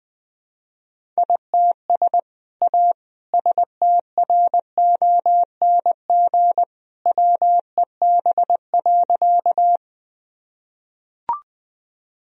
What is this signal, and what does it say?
Morse code, heard as a keyed tone, about 20 wpm, 700 Hz. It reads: ITS A STRONG WEB.
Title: Sentences from Top 1000 Words - Review Part 4 - 20wpm